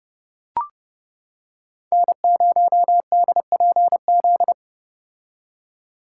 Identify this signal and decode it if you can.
Morse code, heard as a keyed tone, about 29 wpm, 700 Hz. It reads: N0DPZ